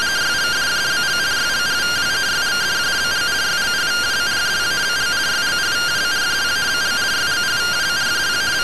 Two-tone teleprinter signal, about 170 Hz shift.